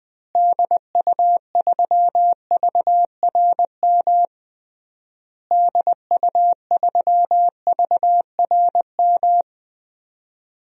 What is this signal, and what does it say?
Morse code, heard as a keyed tone, about 21 wpm, 700 Hz. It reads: DU3VRM DU3VRM
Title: DXCC-100 Call Signs - 6 Characters - Repeated 3x - 20wpm